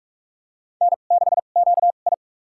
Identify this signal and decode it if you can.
Morse code, heard as a keyed tone, about 47 wpm, spaced as for 33 wpm, 700 Hz. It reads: N6QI
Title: USA Call Signs - 4 Characters - 2x Character Spacing - 45wpm